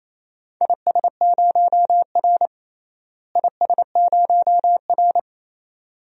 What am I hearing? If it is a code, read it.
Morse code, heard as a keyed tone, about 28 wpm, 700 Hz. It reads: IS0R IS0R
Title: DXCC-100 Call Signs - 4 Characters - Repeated 2x - 28wpm